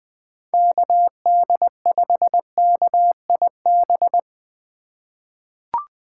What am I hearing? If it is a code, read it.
Morse code, heard as a keyed tone, about 20 wpm, 700 Hz. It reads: KD5KIB
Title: USA Call Signs - 6 Characters - 8x Character Spacing - 20wpm